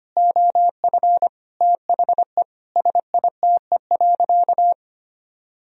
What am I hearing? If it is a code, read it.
Morse code, heard as a keyed tone, about 25 wpm, 700 Hz. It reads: OF THE SITE.
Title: Sentences from Top 1000 Words - Review Part 4 - 25wpm